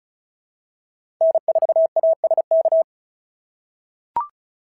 Morse code, heard as a keyed tone, about 36 wpm, 690 Hz. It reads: N4ASK
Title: SOTA QSO Practice 35wpm